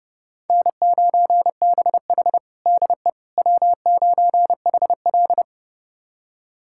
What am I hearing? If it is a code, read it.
Morse code, heard as a keyed tone, about 30 wpm, 700 Hz. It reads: N9BH DE W9HL
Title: USA Callsign DE Callsign - 4 Characters - 30wpm